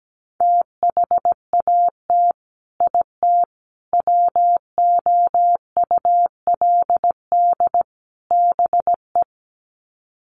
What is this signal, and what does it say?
Morse code, heard as a keyed tone, about 17 wpm, 700 Hz. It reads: THAT IT WOULD BE
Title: Sets of 4 Words 17wpm